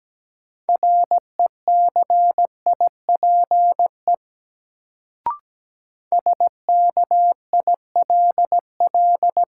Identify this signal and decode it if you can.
Morse code, heard as a keyed tone, about 17 wpm, 700 Hz. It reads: RECIPE SKILL
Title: Sentences from Top 1200 Words - New Word Drill Alt - 17wpm